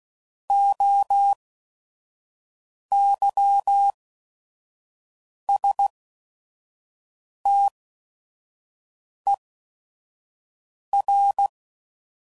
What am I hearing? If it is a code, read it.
Morse code, heard as a keyed tone, about 16 wpm, spaced as for 5 wpm, 800 Hz. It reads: OYSTER
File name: Prefixes / By First Letter / Os